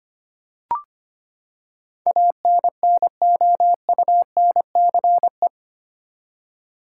Morse code, heard as a keyed tone, about 25 wpm, 700 Hz. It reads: ANNOUNCE
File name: Eight Letter Words - 25wpm